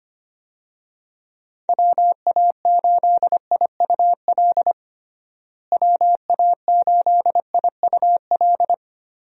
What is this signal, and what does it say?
Morse code, heard as a keyed tone, about 25 wpm, 700 Hz. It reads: WA8IUL WA8IUL
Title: USA Call Signs - 6 Characters - Repeated 3x - 25wpm